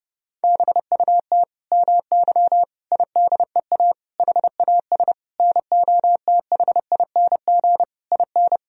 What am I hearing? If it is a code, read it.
Morse code, heard as a keyed tone, about 30 wpm, 700 Hz. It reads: BUT MY IDEA HAS NOTHING IN
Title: Sentences from Top 1000 Words - Review Part 3 - 30wpm